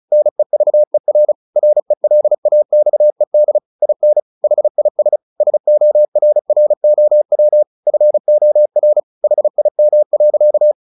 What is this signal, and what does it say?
Morse code, heard as a keyed tone, about 36 wpm, 600 Hz. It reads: NEVER RELAXED IN HIS SORROW FOR HIM.
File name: the old man and the sea 2 35wpm